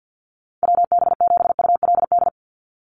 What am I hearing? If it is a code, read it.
Morse code, heard as a keyed tone, about 48 wpm, 700 Hz. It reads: WB7ULD
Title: NA SOTA Chaser Callsigns 50wpm